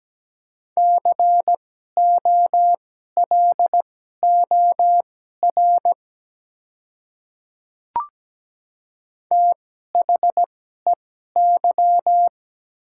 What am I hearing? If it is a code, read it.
Morse code, heard as a keyed tone, about 17 wpm, spaced as for 12 wpm, 700 Hz. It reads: COLOR THEY
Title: 2x Character Spacing - Top 300 Words - 17wpm